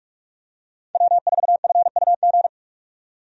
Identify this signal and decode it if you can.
Morse code, heard as a keyed tone, about 46 wpm, 700 Hz. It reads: W4FUG
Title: USA Call Signs - 5 Characters - 45wpm